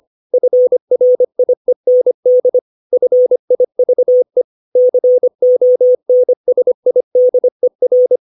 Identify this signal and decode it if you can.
Morse code, heard as a keyed tone, about 25 wpm, 500 Hz. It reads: FRIEND FIVE CONSIDER
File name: 632 words / 632 Words 25WPM